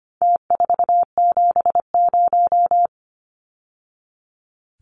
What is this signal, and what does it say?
Morse code, heard as a keyed tone, about 25 wpm, 700 Hz. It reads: T470
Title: Numbers - Contest Style - 4-Digit 25wpm